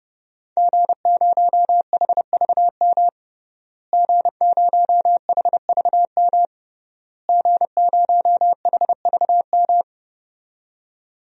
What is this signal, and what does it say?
Morse code, heard as a keyed tone, about 32 wpm, 700 Hz. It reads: G0HVM G0HVM G0HVM
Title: DXCC-100 Call Signs - 5 Characters - Repeated 3x - 30wpm